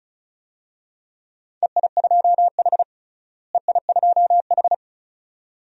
Morse code, heard as a keyed tone, about 36 wpm, 700 Hz. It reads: EI2H EI2H
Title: DXCC-100 Call Signs - 4 Characters - Repeated 2x - 35wpm